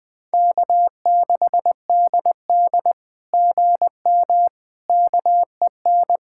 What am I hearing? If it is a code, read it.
Morse code, heard as a keyed tone, about 20 wpm, 700 Hz. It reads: K6DD GM KEN
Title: SOTA QSO Practice 20wpm